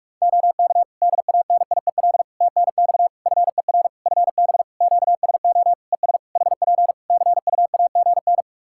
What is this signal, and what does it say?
Morse code, heard as a keyed tone, about 46 wpm, 700 Hz. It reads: OK DANIEL TNX FER FB QSO ES HP CUAGN
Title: K1USN SST QSO Practice 45wpm